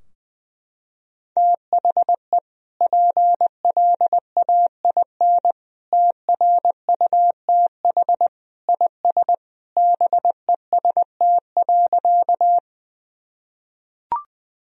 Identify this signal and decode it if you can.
Morse code, heard as a keyed tone, about 20 wpm, 700 Hz. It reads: THE PLAIN TRUTH IS BEST.
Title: Sentences from Top 1000 Words - Review Part 2 - 20wpm